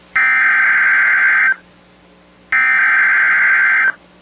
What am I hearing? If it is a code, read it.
Emergency Alert System (SAME) tones (header bursts), heard as harsh buzzing bursts. It is heard from 0.2 to 3.8 s.